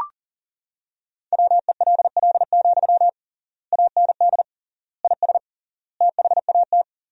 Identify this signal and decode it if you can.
Morse code, heard as a keyed tone, about 40 wpm, 700 Hz. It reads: WELL, AND IS THAT
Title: 3x Word-Spacing - Sentences from Top 200 Words - 40wpm